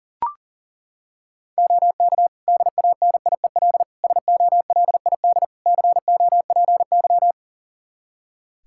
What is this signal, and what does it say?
Morse code, heard as a keyed tone, about 40 wpm, 700 Hz. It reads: OK DANIEL SOLID COPY
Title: K1USN SST QSO Practice 40wpm